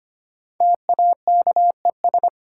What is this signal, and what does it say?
Morse code, heard as a keyed tone, about 25 wpm, 700 Hz. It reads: TAKES
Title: Sentences from Top 1000 Words - Review Part 6 - New Word Drill - 25wpm